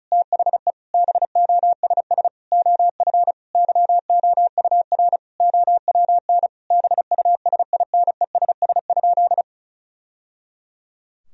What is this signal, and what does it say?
Morse code, heard as a keyed tone, about 36 wpm, 700 Hz. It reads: THE BOSS OF YOUR OWN BUSINESS?
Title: Sentences from Top 1000 Words - Review Part 4 - 35wpm